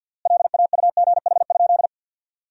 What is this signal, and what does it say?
Morse code, heard as a keyed tone, about 52 wpm, 700 Hz. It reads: LAUGH?
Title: 3x Word-Spacing - Sentences from Top 500 Words - 50wpm